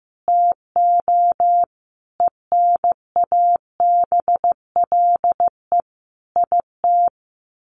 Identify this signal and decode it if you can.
Morse code, heard as a keyed tone, about 15 wpm, 700 Hz. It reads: TO ENABLE IT
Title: Sentences from Top 1500 Words 15wpm